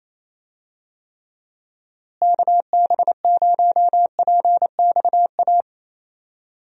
Morse code, heard as a keyed tone, about 28 wpm, 700 Hz. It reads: KB0PXA
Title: USA Call Signs - 6 Characters - Repeated 3x - 28wpm